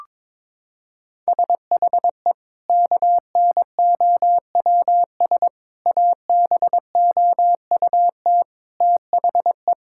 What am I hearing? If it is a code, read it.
Morse code, heard as a keyed tone, about 22 wpm, 700 Hz. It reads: SHE KNOWS ABOUT THE
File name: practice / 22wpm / Sentences with 2-Word Phrases - 22wpm